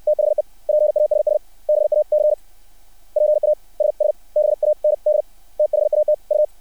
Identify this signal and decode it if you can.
Morse code, heard as a keyed tone, about 60 wpm, spaced as for 49 wpm, 600 Hz. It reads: THE PASS BAC PASSBAND THAT U
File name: archived / CRSnet